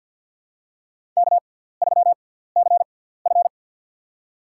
Morse code, heard as a keyed tone, about 53 wpm, spaced as for 17 wpm, 700 Hz. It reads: K3CF
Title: USA Call Signs - 4 Characters - 6x Character Spacing - 50wpm